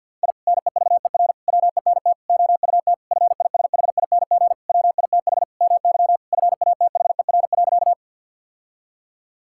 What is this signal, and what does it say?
Morse code, heard as a keyed tone, about 50 wpm, 700 Hz. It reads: I NEVER WENT OUT FISHING WITH MY FATHER.
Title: Sentences from Top 1000 Words - Review Part 5 - 50wpm